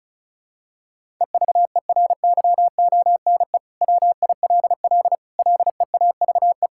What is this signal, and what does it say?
Morse code, heard as a keyed tone, about 36 wpm, 700 Hz. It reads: EVERYONE WILL LEAVE
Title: Sentences from Top 1000 Words - Review Part 5 - 35wpm